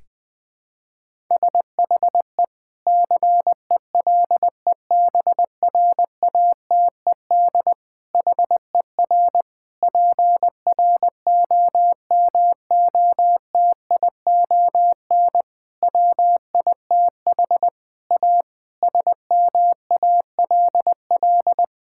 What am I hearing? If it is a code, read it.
Morse code, heard as a keyed tone, about 20 wpm, 700 Hz. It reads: SHE CELEBRATED HER PROMOTION WITH A SMALL
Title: Difficult Sentences with Hints - 20wpm